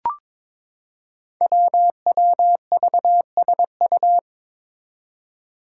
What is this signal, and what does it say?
Morse code, heard as a keyed tone, about 22 wpm, 700 Hz. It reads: WWVSU
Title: Five Challenging Letter-Number Combinations 22wpm